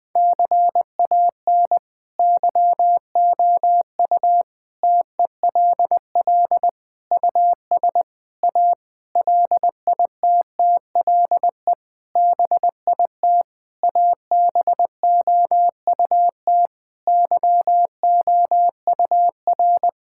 Morse code, heard as a keyed tone, about 20 wpm, 700 Hz. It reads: CAN YOU TELL US A LITTLE BIT ABOUT YOUR
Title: Sentences with 8-Word Phrases - 20wpm